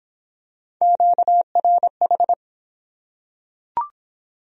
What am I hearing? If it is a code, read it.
Morse code, heard as a keyed tone, about 21 wpm, 700 Hz. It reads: QRH
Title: Speed-Racing Q-Codes - 17wpm